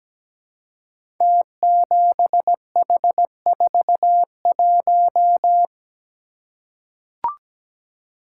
Morse code, heard as a keyed tone, about 17 wpm, 700 Hz. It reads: T7H41
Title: Code Groups - 5 Letter:Numbers - 17wpm